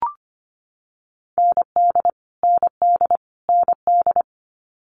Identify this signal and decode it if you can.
Morse code, heard as a keyed tone, about 25 wpm, 700 Hz. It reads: ND ND ND
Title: Multi-letter Phonemes Repeated 3x 25wpm